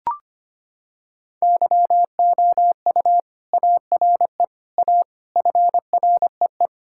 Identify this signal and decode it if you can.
Morse code, heard as a keyed tone, about 25 wpm, 700 Hz. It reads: YOU ARE A FREE
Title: Sentences from Top 1600 Words 25wpm